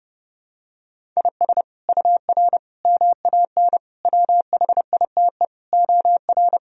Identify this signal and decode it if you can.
Morse code, heard as a keyed tone, about 30 wpm, 700 Hz. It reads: IS UR MAN WHITE OR
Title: Sentences from Top 400 Words and Abbreviations 30wpm